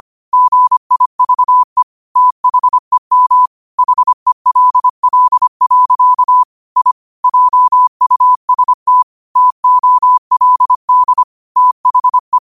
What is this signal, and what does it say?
Morse code, heard as a keyed tone, about 25 wpm, 1000 Hz. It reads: GIVE THEM HELL. I JUST TOLD THE
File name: qotdfiles / 25wpm